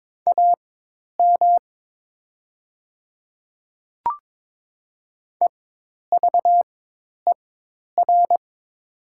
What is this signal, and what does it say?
Morse code, heard as a keyed tone, about 22 wpm, spaced as for 10 wpm, 700 Hz. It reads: AM EVER
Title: 4x Character Spacing - Top 500 Words - 22wpm